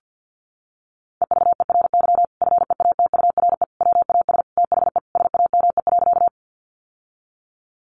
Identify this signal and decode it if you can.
Morse code, heard as a keyed tone, about 50 wpm, 700 Hz. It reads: EVERY FEATURE WAS THE SAME.
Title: Sentences from Top 800 Words 50wpm